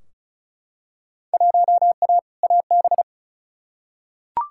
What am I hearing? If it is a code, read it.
Morse code, heard as a keyed tone, about 36 wpm, 700 Hz. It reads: 1A AB
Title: Field Day Exchanges - 35wpm